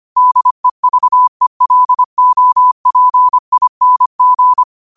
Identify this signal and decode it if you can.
Morse code, heard as a keyed tone, about 25 wpm, 1000 Hz. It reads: DEVELOPING
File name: qotdfiles / 25wpm